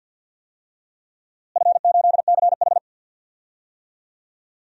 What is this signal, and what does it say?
Morse code, heard as a keyed tone, about 53 wpm, 700 Hz. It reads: F8CH
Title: DXCC-100 Call Signs - 4 Characters - Repeated 3x - 50wpm